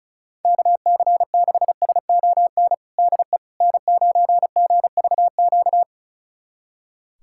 Morse code, heard as a keyed tone, about 36 wpm, 700 Hz. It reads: KC6SON DE N9GVQ
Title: USA Callsign DE Callsign 35wpm